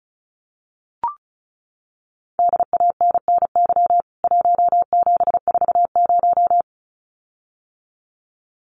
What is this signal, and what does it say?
Morse code, heard as a keyed tone, about 34 wpm, 700 Hz. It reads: DANNY 1740
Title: CWT Exchange 35wpm